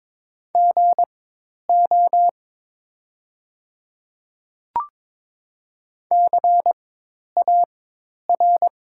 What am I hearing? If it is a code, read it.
Morse code, heard as a keyed tone, about 22 wpm, spaced as for 10 wpm, 690 Hz. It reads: GO CAR
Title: 4x Character Spacing - Top 300 Words - 22wpm